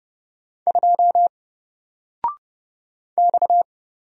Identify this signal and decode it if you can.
Morse code, heard as a keyed tone, about 30 wpm, 700 Hz. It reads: 2 X
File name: Single-Letter-Number-30wpm